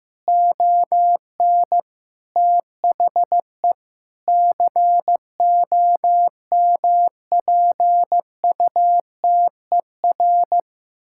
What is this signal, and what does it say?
Morse code, heard as a keyed tone, about 15 wpm, 700 Hz. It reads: ON THE COMPUTER
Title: Sentences from Top 1000 Words - Review Part 6 - 15wpm